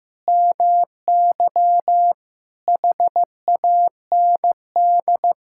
Morse code, heard as a keyed tone, about 15 wpm, 700 Hz. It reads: MY HAND
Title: Sentences from Top 1000 Words - Review Part 6 - 15wpm